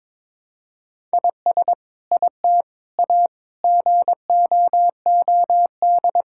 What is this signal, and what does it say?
Morse code, heard as a keyed tone, about 21 wpm, 700 Hz. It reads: IS IT A GOOD
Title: Sentences from Top 1000 Words - Review Part 6 - 22wpm